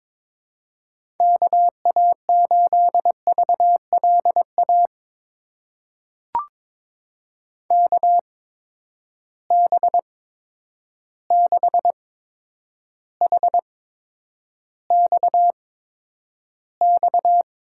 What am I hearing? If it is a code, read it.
Morse code, heard as a keyed tone, about 22 wpm, spaced as for 9 wpm, 700 Hz. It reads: KA8VLA KB6HXX
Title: USA Call Signs - 6 Characters - 8x Character Spacing - 22wpm